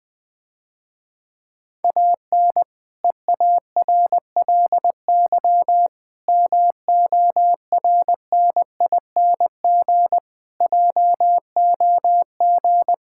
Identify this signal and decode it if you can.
Morse code, heard as a keyed tone, about 20 wpm, 700 Hz. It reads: AN EARLY MORNING JOG
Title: Difficult Sentences with Hints - 20wpm